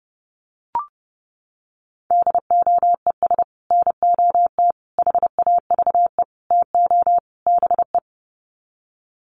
Morse code, heard as a keyed tone, about 30 wpm, 700 Hz. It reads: DOES NOT HAVE TO BE
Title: Sets of 5 Words 30wpm